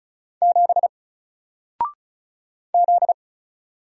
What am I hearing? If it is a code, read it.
Morse code, heard as a keyed tone, about 36 wpm, 700 Hz. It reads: 7 Z
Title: Single Letter-Number 35wpm